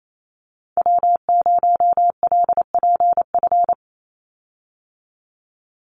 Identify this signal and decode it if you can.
Morse code, heard as a keyed tone, about 28 wpm, 700 Hz. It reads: W0LPF
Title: Call Signs Repeated 3x - 28wpm